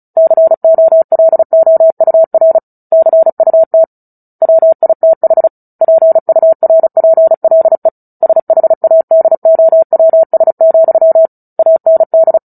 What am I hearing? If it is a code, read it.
Morse code, heard as a keyed tone, about 34 wpm, 650 Hz. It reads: COLOUR CUT WITH PURPLE SHADOWS, AND
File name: Book Two / 09-Chapter 9 - 35 WPM